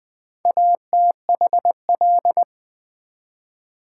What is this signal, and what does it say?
Morse code, heard as a keyed tone, about 21 wpm, 700 Hz. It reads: ATHL
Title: Build A Word - 2 Letters At A Time - 8-Letter Words - 20wpm